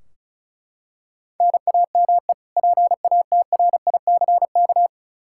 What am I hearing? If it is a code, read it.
Morse code, heard as a keyed tone, about 36 wpm, 700 Hz. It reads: NAME PATRICK
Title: K1USN SST QSO Practice 35wpm